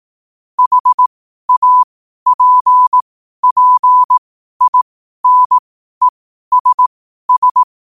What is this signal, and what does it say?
Morse code, heard as a keyed tone, about 18 wpm, spaced as for 12 wpm, 1000 Hz. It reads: HAPPINESS
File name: qotdfiles / 12wpm